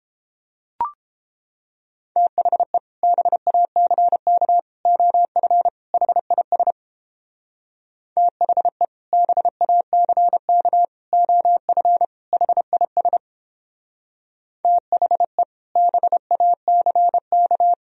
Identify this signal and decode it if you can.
Morse code, heard as a keyed tone, about 29 wpm, 700 Hz. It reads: THE BACK OF HIS THE BACK OF HIS THE BACK